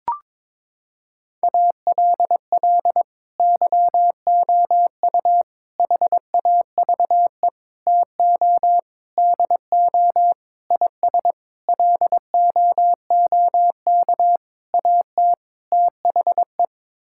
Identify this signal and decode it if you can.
Morse code, heard as a keyed tone, about 22 wpm, 700 Hz. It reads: ALL YOU HAVE TO DO IS LOOK AT THE
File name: Sets of 9 Words - 22wpm